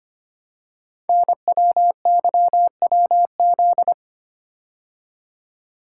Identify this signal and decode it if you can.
Morse code, heard as a keyed tone, about 25 wpm, 700 Hz. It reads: NWYWZ
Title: Code Groups - 5 Letters - 25wpm